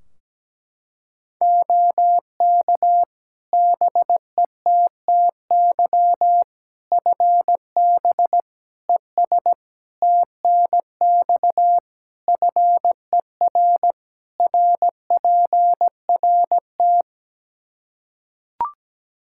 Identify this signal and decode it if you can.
Morse code, heard as a keyed tone, about 17 wpm, 700 Hz. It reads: OK BETTY FB ES TNX FER RPRT
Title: Practice QSO - Exchange 2 - Info - 17wpm